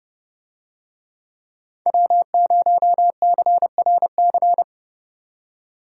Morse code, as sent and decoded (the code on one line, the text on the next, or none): .-- ----- -.-. .-. -.-.
W0CRC